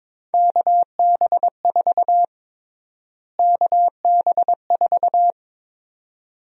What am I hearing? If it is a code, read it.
Morse code, heard as a keyed tone, about 21 wpm, 700 Hz. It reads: KB4 KB4
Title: USA Call Sign Prefixes - Repeated 2x - 22wpm